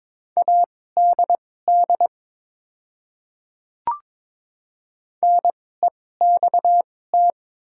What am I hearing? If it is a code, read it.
Morse code, heard as a keyed tone, about 23 wpm, spaced as for 16 wpm, 700 Hz. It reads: ADD NEXT
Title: 2x Character Spacing - Top 500 Words - 22wpm